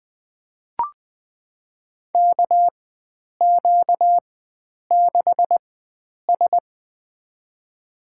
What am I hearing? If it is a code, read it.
Morse code, heard as a keyed tone, about 20 wpm, spaced as for 9 wpm, 700 Hz. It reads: KQ6S
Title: USA Call Signs - 4 Characters - 4x Character Spacing - 20wpm